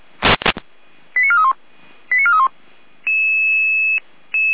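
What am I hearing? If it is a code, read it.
Five-tone selective call (ZVEI), heard as quick runs of five tones, tones of about 70 ms. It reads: 89431 89431
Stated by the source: ZVEI five-tone selective call received on an SDR.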